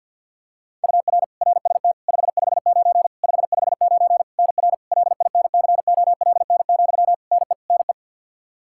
Morse code, heard as a keyed tone, about 51 wpm, 700 Hz. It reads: UR RST 559 559 NR LINCOLN, NE NE
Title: Practice QSO - Exchange 1 - Beginning - 50wpm